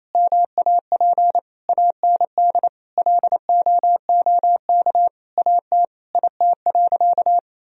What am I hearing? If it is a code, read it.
Morse code, heard as a keyed tone, about 28 wpm, 700 Hz. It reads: MAP AND LOOK AT IT.
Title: Sentences from Top 1000 Words - Review Part 2 - 28wpm